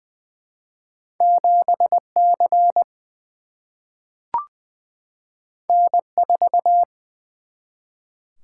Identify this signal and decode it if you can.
Morse code, heard as a keyed tone, about 20 wpm, 700 Hz. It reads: 7 C N 4
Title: Single-Letter Single-Number 20wpm